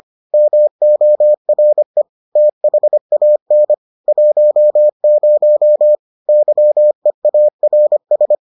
Morse code, heard as a keyed tone, about 25 wpm, 600 Hz. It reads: MORE THAN 10 YEARS